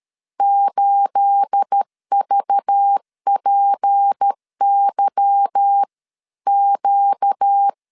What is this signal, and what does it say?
Morse code, heard as a keyed tone, about 13 wpm, spaced as for 11 wpm, 800 Hz. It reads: 8VPYQ